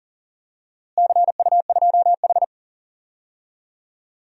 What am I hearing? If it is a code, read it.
Morse code, heard as a keyed tone, about 40 wpm, 700 Hz. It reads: CU2H
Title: DXCC-100 Call Signs - 4 Characters - Repeated 3x - 40wpm